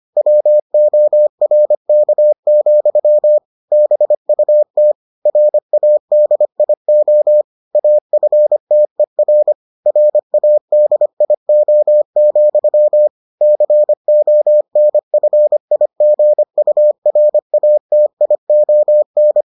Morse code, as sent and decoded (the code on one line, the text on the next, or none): .-- --- .-. -.- --..-- / -... ..- - / .-. .- -.. .. --- / .- ..-. - . .-. / .-. .- -.. .. --- --..-- / -.-. --- -. ..-. .. --. ..- .-. .- - .. --- -.
WORK, BUT RADIO AFTER RADIO, CONFIGURATION